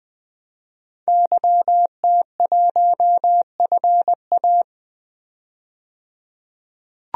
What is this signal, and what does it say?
Morse code, heard as a keyed tone, about 20 wpm, 700 Hz. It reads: YT1FA
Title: DXCC-100 Call Signs - 5 Characters - 20wpm